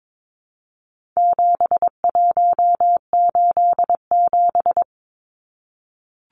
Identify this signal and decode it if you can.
Morse code, heard as a keyed tone, about 22 wpm, 700 Hz. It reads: 7187
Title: Numbers - 4-Digit 22wpm